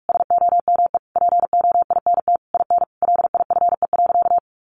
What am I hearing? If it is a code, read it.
Morse code, heard as a keyed tone, about 44 wpm, 700 Hz. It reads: SOME POINT IN LIFE.
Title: Sentences from Top 1200 Words 45wpm